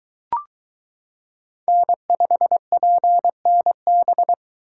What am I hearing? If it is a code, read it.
Morse code, heard as a keyed tone, about 22 wpm, 700 Hz. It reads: N5PNB